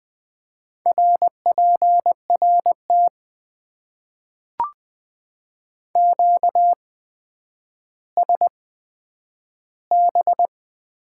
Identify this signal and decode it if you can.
Morse code, heard as a keyed tone, about 20 wpm, spaced as for 10 wpm, 700 Hz. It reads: RPRT QSB
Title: Top 90 QSO Elements - 8x Character Spacing - 20wpm